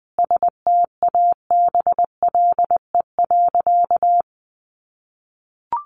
Morse code, heard as a keyed tone, about 20 wpm, 700 Hz. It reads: STABLE.
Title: Sentences from Top 1500 Words 20wpm